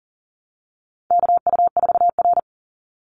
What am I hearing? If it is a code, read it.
Morse code, heard as a keyed tone, about 40 wpm, 700 Hz. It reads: KU4R
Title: NA SOTA Chaser Callsigns 40wpm